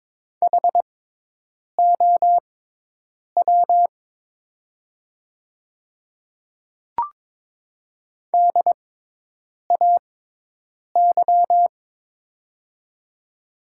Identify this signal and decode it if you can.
Morse code, heard as a keyed tone, about 22 wpm, spaced as for 8 wpm, 700 Hz. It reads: HOW DAY
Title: 6x Character Spacing - Top 100 Words - 22wpm